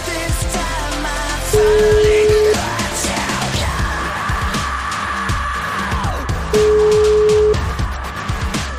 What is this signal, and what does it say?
Ringback tone (Europe, 425 Hz), with noise.